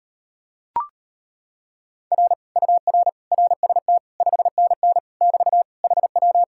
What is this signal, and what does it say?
Morse code, heard as a keyed tone, about 38 wpm, 700 Hz. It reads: R UR RST 5NN <BT> HW